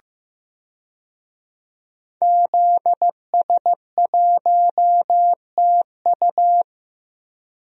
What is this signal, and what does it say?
Morse code, heard as a keyed tone, about 15 wpm, 700 Hz. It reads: ZS1TU
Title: DXCC-100 Call Signs - 5 Characters - 15wpm